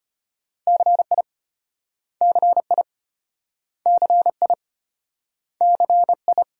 Morse code, heard as a keyed tone, about 33 wpm, 700 Hz. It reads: CI CI CI CI